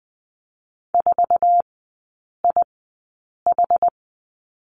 Morse code, heard as a keyed tone, about 20 wpm, spaced as for 8 wpm, 700 Hz. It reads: 4IH
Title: 093 Lesson 16e - Letter c - Three Characters 20wpm